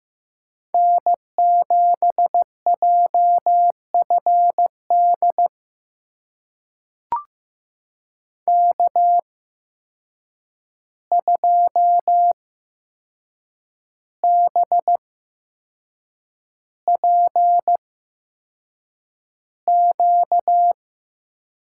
Morse code, heard as a keyed tone, about 15 wpm, spaced as for 6 wpm, 700 Hz. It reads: N7JFD K2BPQ
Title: USA Call Signs - 5 Characters - 8x Character Spacing - 15wpm